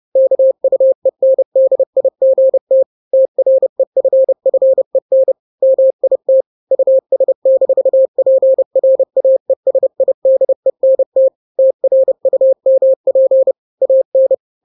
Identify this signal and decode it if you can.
Morse code, heard as a keyed tone, about 30 wpm, 530 Hz. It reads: KUENDIGT TREFFEN MIT US-PRAESIDENT TRUMP AN
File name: daily / 29wpm